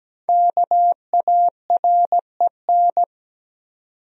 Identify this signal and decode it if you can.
Morse code, heard as a keyed tone, about 17 wpm, 700 Hz. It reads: KAREN
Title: USA Common Names - Max 5 Letters - 17wpm